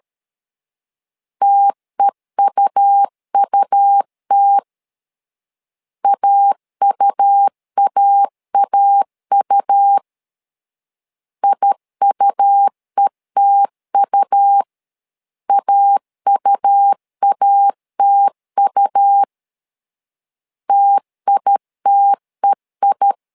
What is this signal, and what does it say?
Morse code, heard as a keyed tone, about 13 wpm, 800 Hz. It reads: TEUUT AUAAU IUETU AUATU TITEI
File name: LEZIONI / 2 UHDN / 2 U 12wpm 1338